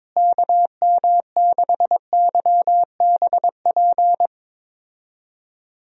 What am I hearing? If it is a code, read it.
Morse code, heard as a keyed tone, about 22 wpm, 700 Hz. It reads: KM6YBP
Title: USA Call Signs - 6 Characters - Repeated 3x - 22wpm